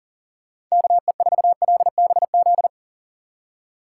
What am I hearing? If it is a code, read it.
Morse code, heard as a keyed tone, about 37 wpm, 700 Hz. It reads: KE4LBZ
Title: USA Call Signs - 6 Characters - 6x Character Spacing - 40wpm